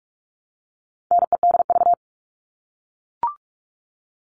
Morse code, heard as a keyed tone, about 44 wpm, 700 Hz. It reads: NEDV
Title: All Cut Numbers - 4-Digit 45wpm